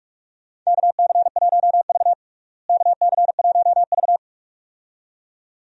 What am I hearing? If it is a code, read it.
Morse code, heard as a keyed tone, about 46 wpm, 700 Hz. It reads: KC1V KC1V
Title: USA Call Signs - 4 Characters - Repeated 2x - 45wpm